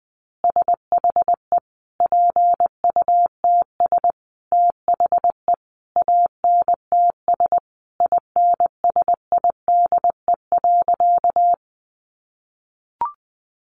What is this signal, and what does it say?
Morse code, heard as a keyed tone, about 20 wpm, 700 Hz. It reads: SHE PUTS THE ANTS INSIDE.
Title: Easy Phrases 20wpm